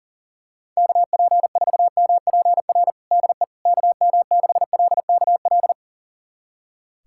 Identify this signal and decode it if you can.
Morse code, heard as a keyed tone, about 40 wpm, 700 Hz. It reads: KP4MPR DE KM6LKL
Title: USA Callsign DE Callsign - 6 Characters - 40wpm